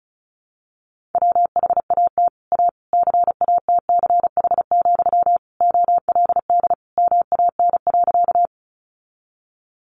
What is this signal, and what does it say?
Morse code, heard as a keyed tone, about 34 wpm, 700 Hz. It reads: WHAT A CATCH, OLD MAN.
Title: Sentences from Top 1500 Words 35wpm